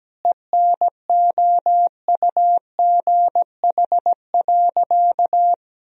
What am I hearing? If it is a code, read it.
Morse code, heard as a keyed tone, about 17 wpm, 700 Hz. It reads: ENOUGH.
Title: 2x Word-Spacing - Sentences from Top 400 Words - 17wpm